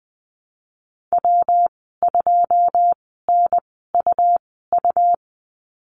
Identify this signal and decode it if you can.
Morse code, heard as a keyed tone, about 20 wpm, spaced as for 15 wpm, 700 Hz. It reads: W2NUU
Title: 193 Lesson 31g - Letter v - Call Signs 20wpm